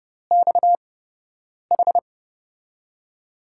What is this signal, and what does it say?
Morse code, heard as a keyed tone, about 32 wpm, spaced as for 8 wpm, 700 Hz. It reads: XH
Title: USA Call Signs - 5 Characters - 8x Character Spacing - 30wpm